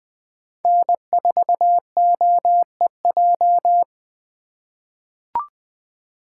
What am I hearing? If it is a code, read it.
Morse code, heard as a keyed tone, about 20 wpm, 700 Hz. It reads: N4OEJ
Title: USA Call Signs - 5 Characters - 8x Character Spacing - 20wpm